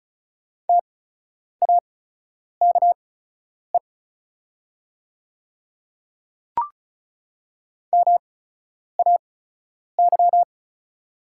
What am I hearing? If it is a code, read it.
Morse code, heard as a keyed tone, about 36 wpm, spaced as for 10 wpm, 690 Hz. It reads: TAKE MAY
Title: 8x Character Spacing - Top 200 Words - 35wpm